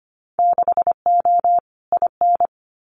Morse code, heard as a keyed tone, about 25 wpm, 700 Hz. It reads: 6O IN
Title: Winter Field Day Exchanges Encore 25wpm